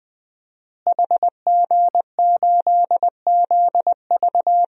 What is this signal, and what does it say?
Morse code, heard as a keyed tone, about 20 wpm, 700 Hz. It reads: HG8ZV
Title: DXCC-100 Call Signs - 5 Characters - Repeated 3x - 20wpm